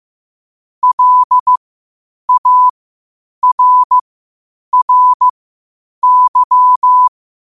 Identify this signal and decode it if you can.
Morse code, heard as a keyed tone, about 15 wpm, spaced as for 8 wpm, 1000 Hz. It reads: LARRY
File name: qotdfiles / 8wpm